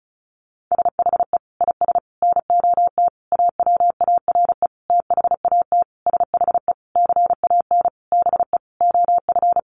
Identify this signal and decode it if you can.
Morse code, heard as a keyed tone, about 34 wpm, 700 Hz. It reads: SHE IS NOT AWARE THAT SHE CAN BE OF
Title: Sentences from Top 1200 Words 35wpm